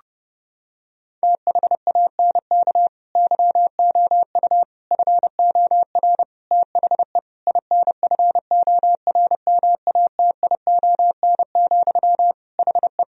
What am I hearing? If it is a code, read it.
Morse code, heard as a keyed tone, about 30 wpm, 700 Hz. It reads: THANK YOU FOR THE INFORMATION, HE
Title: Sentences from Top 1000 Words - Review Part 3 - 30wpm